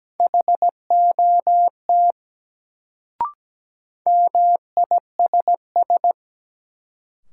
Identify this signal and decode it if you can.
Morse code, heard as a keyed tone, about 17 wpm, 700 Hz. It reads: HOT MISS
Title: Sentences from Top 500 Words - New Word Drill - 17wpm